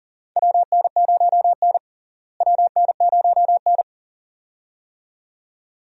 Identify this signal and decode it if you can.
Morse code, heard as a keyed tone, about 40 wpm, 700 Hz. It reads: WN0N WN0N
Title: USA Call Signs - 4 Characters - Repeated 2x - 40wpm